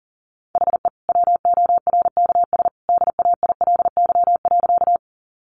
Morse code, heard as a keyed tone, about 40 wpm, 700 Hz. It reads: HE WORKS DAILY.
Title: Sentences from Top 1100 Words 40wpm